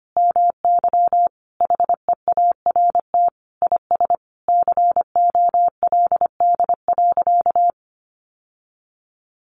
Morse code, heard as a keyed tone, about 25 wpm, 700 Hz. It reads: MY HEART IS COLD.
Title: Sentences from Top 1200 Words 25wpm